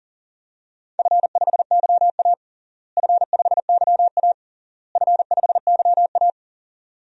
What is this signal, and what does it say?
Morse code, heard as a keyed tone, about 37 wpm, 700 Hz. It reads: F5YA F5YA F5YA
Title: DXCC-100 Call Signs - 4 Characters - Repeated 3x - 40wpm